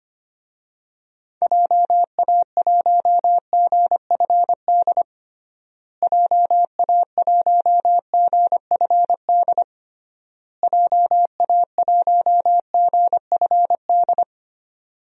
Morse code, heard as a keyed tone, about 25 wpm, 700 Hz. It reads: JA1GFD JA1GFD JA1GFD
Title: DXCC-100 Call Signs - 6 Characters - Repeated 3x - 25wpm